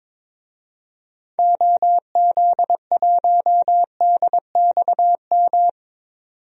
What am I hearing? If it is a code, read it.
Morse code, heard as a keyed tone, about 22 wpm, 700 Hz. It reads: OZ1DXM
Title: DXCC-100 Call Signs - 6 Characters - 22wpm